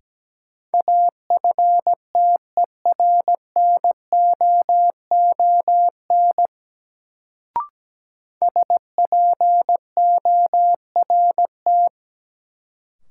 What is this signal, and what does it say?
Morse code, heard as a keyed tone, about 17 wpm, 700 Hz. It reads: AFTERNOON SPORT
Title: Sentences from Top 1500 Words - New Word Drill Alt - 17wpm